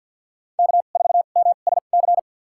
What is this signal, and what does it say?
Morse code, heard as a keyed tone, about 53 wpm, spaced as for 37 wpm, 700 Hz. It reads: K4MSC
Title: USA Call Signs - 5 Characters - 2x Character Spacing - 50wpm